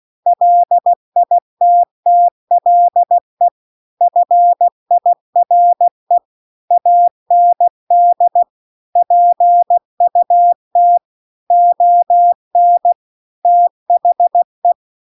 Morse code, heard as a keyed tone, about 16 wpm, 700 Hz. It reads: LITTLE FIRE AND PUT ON THE
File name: CW-The-Big-Twoheaded-River-2-16WPM